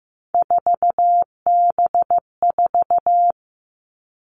Morse code, heard as a keyed tone, about 15 wpm, 700 Hz. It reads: 4B4
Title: Three Challenging Letter-Number Combinations 15wpm